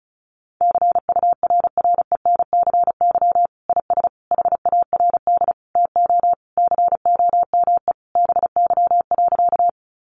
Morse code, heard as a keyed tone, about 34 wpm, 700 Hz. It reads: CURRENCY IS HARD TO COME BY.
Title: Sentences from Top 1500 Words 35wpm